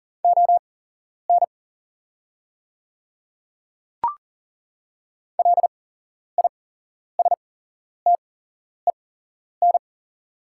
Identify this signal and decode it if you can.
Morse code, heard as a keyed tone, about 39 wpm, spaced as for 11 wpm, 700 Hz. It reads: ON LISTEN
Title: 8x Character Spacing - Top 500 Words - 40wpm